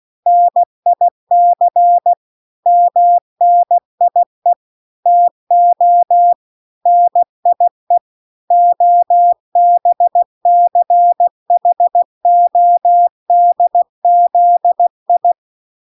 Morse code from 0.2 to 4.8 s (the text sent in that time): NIC MNIE